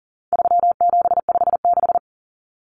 Morse code, heard as a keyed tone, about 40 wpm, 700 Hz. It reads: 3756
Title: Common Cut Numbers - 4-Digit 40wpm